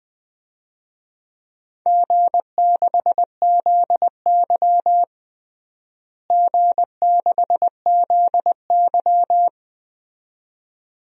Morse code, heard as a keyed tone, about 20 wpm, 700 Hz. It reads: G6ZY G6ZY
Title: DXCC-100 Call Signs - 4 Characters - Repeated 2x - 20wpm